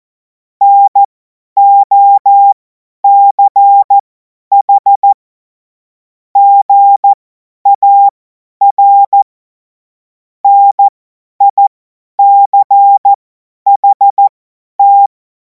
Morse code, heard as a keyed tone, about 14 wpm, spaced as for 10 wpm, 800 Hz. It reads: NOCH GAR NICHT
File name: Morsekurs / Witze / Witz 05. Mai 2026 50 F70